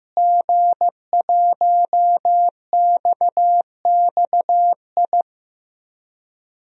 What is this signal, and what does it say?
Morse code, heard as a keyed tone, about 15 wpm, 700 Hz. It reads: G1XXI
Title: DXCC-100 Call Signs - 5 Characters - Repeated 2x - 15wpm